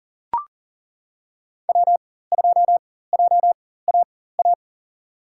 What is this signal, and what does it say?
Morse code, heard as a keyed tone, about 39 wpm, spaced as for 19 wpm, 700 Hz. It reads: W2JAA
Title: USA Call Signs - 5 Characters - 4x Character Spacing - 40wpm